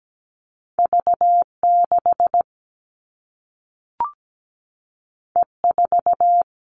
Morse code, heard as a keyed tone, about 17 wpm, 700 Hz. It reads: V6 E4
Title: All Cut Numbers - 2-Digit 17wpm